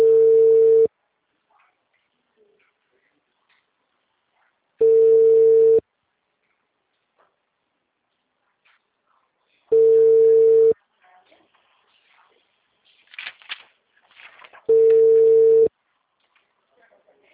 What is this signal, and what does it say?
Ringback tone (China, 450 Hz), with speech.
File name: out- -2025- - - .18939